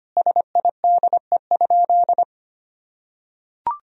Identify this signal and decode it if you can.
Morse code, heard as a keyed tone, about 25 wpm, 700 Hz. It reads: SIDE?
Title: Sentences from Top 1000 Words - Review Part 3 - 25wpm